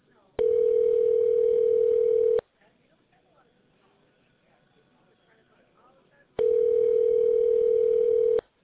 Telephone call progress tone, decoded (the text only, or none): ringback tone